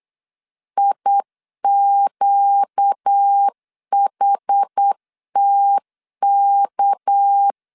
Morse code, heard as a keyed tone, about 8 wpm, 800 Hz. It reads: IQHTK
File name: LEZIONI / 6 PKX0 / 6 PK 8wpm 3003